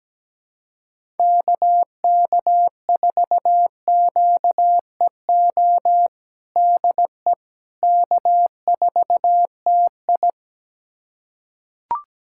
Morse code, heard as a keyed tone, about 17 wpm, 700 Hz. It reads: KK4QEO DE K4TI
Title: USA Callsign DE Callsign 17wpm